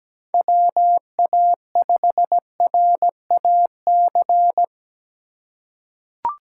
Morse code, heard as a keyed tone, about 17 wpm, 700 Hz. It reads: WA5RAC
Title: USA Call Signs - 6 Characters - 2x Character Spacing - 17wpm